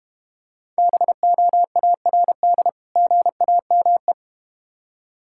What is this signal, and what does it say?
Morse code, heard as a keyed tone, about 31 wpm, 700 Hz. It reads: BOARD GAME
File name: practice / 25wpm / Difficult Words - Speed-Racing with Hints - 25wpm